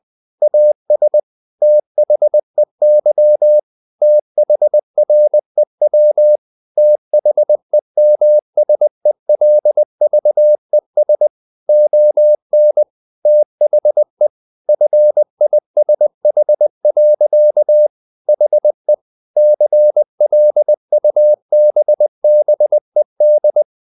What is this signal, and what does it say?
Morse code, heard as a keyed tone, about 20 wpm, 600 Hz. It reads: AS THEY THREW THEMSELVES ON THE FISH. HE CLUBBED
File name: the old man and the sea part 2 20wpm